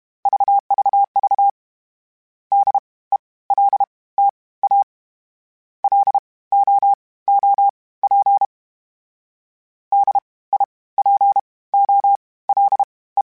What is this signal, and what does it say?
Morse code, heard as a keyed tone, about 32 wpm, spaced as for 19 wpm, 800 Hz. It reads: VVV DELTA LOOP DIPOLE
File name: (spaced) / FromSMs spaced mp3s / Antennas sp@30